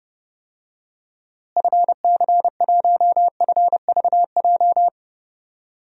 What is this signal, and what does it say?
Morse code, heard as a keyed tone, about 32 wpm, 700 Hz. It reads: FC1FVJ
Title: DXCC-100 Call Signs - 6 Characters - Repeated 2x - 30wpm